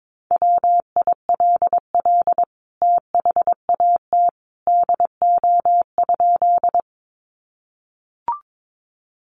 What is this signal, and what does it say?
Morse code, heard as a keyed tone, about 22 wpm, 700 Hz. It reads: WILL THAT DO?
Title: Sentences from Top 100 Words 22wpm